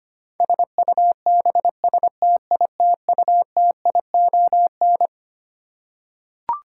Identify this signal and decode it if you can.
Morse code, heard as a keyed tone, about 25 wpm, 700 Hz. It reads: SUBSTITUTION
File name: Twelve Letter Words - 25wpm